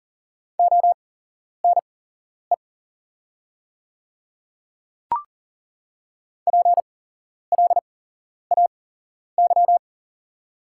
Morse code, heard as a keyed tone, about 39 wpm, spaced as for 11 wpm, 700 Hz. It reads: ONE PLAY
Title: 8x Character Spacing - Top 300 Words - 40wpm